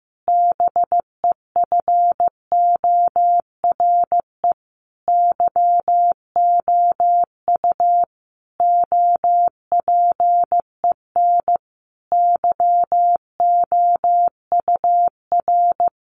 Morse code, heard as a keyed tone, about 15 wpm, 700 Hz. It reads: BEFORE YOU OPEN YOUR
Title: Sentences from Top 1500 Words 15wpm